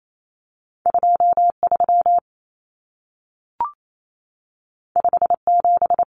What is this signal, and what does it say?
Morse code, heard as a keyed tone, about 28 wpm, 700 Hz. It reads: 23 57
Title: Common Cut Numbers - 2-Digit 28wpm